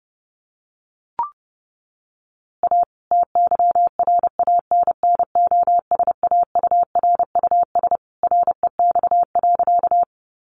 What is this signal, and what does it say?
Morse code, heard as a keyed tone, about 30 wpm, 700 Hz. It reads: A TYRANNOSAURUS REX.
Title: A Sound Of Thunder 30wpm